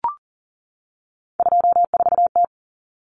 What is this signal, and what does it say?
Morse code, heard as a keyed tone, about 40 wpm, 700 Hz. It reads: 24T
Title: Common Cut Numbers - 3-Digit 40wpm